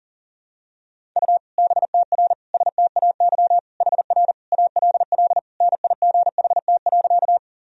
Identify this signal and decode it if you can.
Morse code, heard as a keyed tone, about 40 wpm, 700 Hz. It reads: U BTR STAY HR ALL NIGHT.
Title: Sentences from Top 500 Words and Abbreviations 40wpm